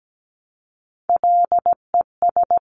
Morse code, heard as a keyed tone, about 17 wpm, 700 Hz. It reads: LES
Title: CWT Exchange 17wpm